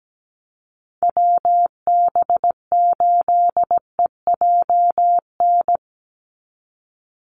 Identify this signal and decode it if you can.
Morse code, heard as a keyed tone, about 17 wpm, 700 Hz. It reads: WB8EJN
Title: Call Signs Repeated 2x - 17wpm